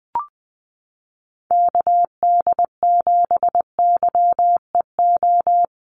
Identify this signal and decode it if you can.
Morse code, heard as a keyed tone, about 20 wpm, 700 Hz. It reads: KD7YEO
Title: USA Call Signs Repeated 2x 20wpm